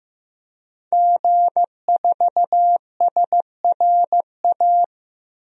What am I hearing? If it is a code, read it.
Morse code, heard as a keyed tone, about 15 wpm, 700 Hz. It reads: G4SRA
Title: DXCC-100 Call Signs - 5 Characters - 15wpm